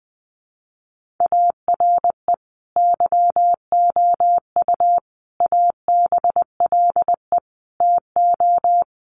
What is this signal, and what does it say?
Morse code, heard as a keyed tone, about 20 wpm, 700 Hz. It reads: ARE YOU ABLE TO
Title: Sentences from Top 1400 Words 20wpm